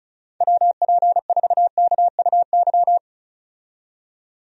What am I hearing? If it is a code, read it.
Morse code, heard as a keyed tone, about 34 wpm, 700 Hz. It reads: WP4KUY
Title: USA Call Signs - 6 Characters - 35wpm